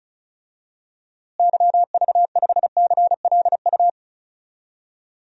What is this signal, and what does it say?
Morse code, heard as a keyed tone, about 36 wpm, 700 Hz. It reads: YV5CLU
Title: DXCC-100 Call Signs - 6 Characters - Repeated 2x - 35wpm